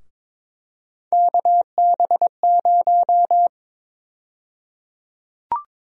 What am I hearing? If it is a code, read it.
Morse code, heard as a keyed tone, about 22 wpm, 700 Hz. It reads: KB0
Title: USA Call Sign Prefixes - Repeated 2x - 22wpm